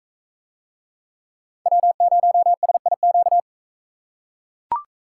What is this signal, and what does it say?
Morse code, heard as a keyed tone, about 42 wpm, 700 Hz. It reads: W0SIQ